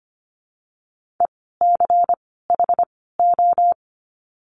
Morse code, heard as a keyed tone, about 25 wpm, 700 Hz. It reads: E C H O
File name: course / 25:10wpm / 094 Lesson 16f - Letter c - Words 25wpm